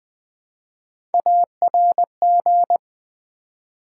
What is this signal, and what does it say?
Morse code, heard as a keyed tone, about 20 wpm, 700 Hz. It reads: ARG
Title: Build A Word - 1 Letter At A Time - 5-Letter Words - 20wpm